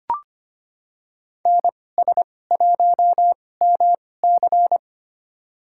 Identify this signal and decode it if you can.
Morse code, heard as a keyed tone, about 25 wpm, spaced as for 18 wpm, 700 Hz. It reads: NS1MC
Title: USA Call Signs - 5 Characters - 2x Character Spacing - 25wpm